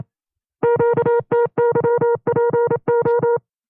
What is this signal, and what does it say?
Morse code, heard as a keyed tone, about 27 wpm, 470 Hz. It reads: QTYPO
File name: FreeFiles / CW-practice / 25WPM-side1